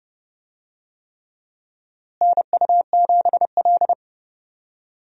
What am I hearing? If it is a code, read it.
Morse code, heard as a keyed tone, about 30 wpm, 700 Hz. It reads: NU7L
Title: USA Call Signs - 4 Characters - Repeated 3x - 30wpm